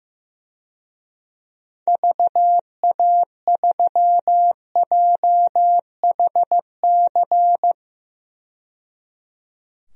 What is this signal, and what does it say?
Morse code, heard as a keyed tone, about 15 wpm, 700 Hz. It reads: VA3JHC
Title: DXCC-100 Call Signs - 6 Characters - 15wpm